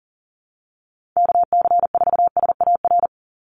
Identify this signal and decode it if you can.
Morse code, heard as a keyed tone, about 40 wpm, 700 Hz. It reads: KC4SAR
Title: USA Call Signs 40wpm